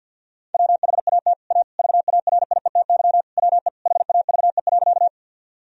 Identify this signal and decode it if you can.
Morse code, heard as a keyed tone, about 50 wpm, 700 Hz. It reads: WHAT A VARIETY WE HAVE.
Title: Sentences from Top 1000 Words - Review Part 4 - 50wpm